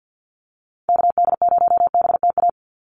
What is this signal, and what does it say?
Morse code, heard as a keyed tone, about 49 wpm, 700 Hz. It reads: KD0BTA
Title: USA Call Signs 50wpm